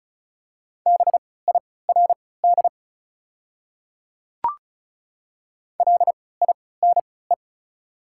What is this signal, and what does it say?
Morse code, heard as a keyed tone, about 36 wpm, spaced as for 20 wpm, 700 Hz. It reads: BIRD LINE
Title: 3x Character Spacing - Top 500 Words - 35wpm